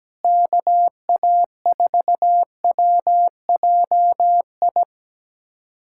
Morse code, heard as a keyed tone, about 17 wpm, 700 Hz. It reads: KA4WJI
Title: USA Callsign DE Callsign - 6 Characters - 17wpm